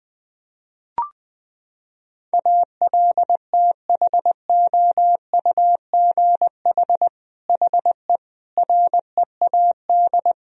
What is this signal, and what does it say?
Morse code, heard as a keyed tone, about 21 wpm, 700 Hz. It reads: ALTHOUGH HE READ
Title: Sentences from Top 1000 Words - Review Part 3 - 20wpm